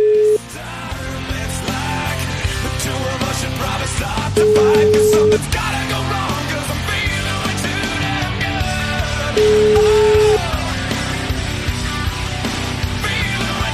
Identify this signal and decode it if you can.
Ringback tone (Europe, 425 Hz), with noise.